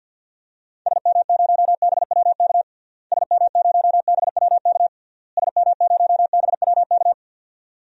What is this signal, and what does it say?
Morse code, heard as a keyed tone, about 50 wpm, 700 Hz. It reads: SM0BWK SM0BWK SM0BWK
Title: DXCC-100 Call Signs - 6 Characters - Repeated 3x - 50wpm